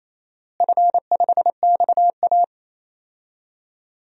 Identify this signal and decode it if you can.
Morse code, heard as a keyed tone, about 28 wpm, 700 Hz. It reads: F5XA
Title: DXCC-100 Call Signs - 4 Characters - Repeated 3x - 28wpm